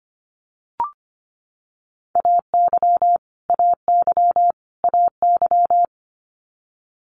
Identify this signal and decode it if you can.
Morse code, heard as a keyed tone, about 25 wpm, 700 Hz. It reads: AY AY AY
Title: Multi-letter Phonemes Repeated 3x 25wpm